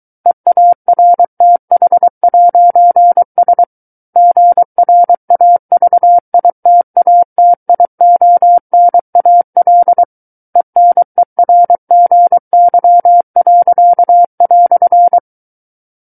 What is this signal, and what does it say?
Morse code, heard as a keyed tone, about 23 wpm, 700 Hz. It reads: EARTH'S GRAVITATIONAL ENERGY."
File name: Book One / 14-Chapter 14 - 23 WPM